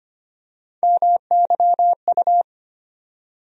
Morse code, heard as a keyed tone, about 25 wpm, 700 Hz. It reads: MYU
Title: USA Call Sign Postfixes - 3 Letters 25wpm